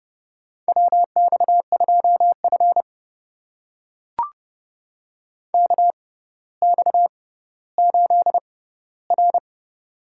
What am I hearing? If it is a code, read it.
Morse code, heard as a keyed tone, about 30 wpm, spaced as for 15 wpm, 700 Hz. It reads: WX2F KX8R
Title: USA Call Signs - 4 Characters - 6x Character Spacing - 30wpm